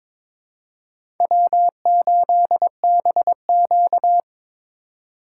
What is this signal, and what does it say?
Morse code, heard as a keyed tone, about 22 wpm, 700 Hz. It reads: W8BQ
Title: USA Call Signs - 4 Characters - 4x Character Spacing - 22wpm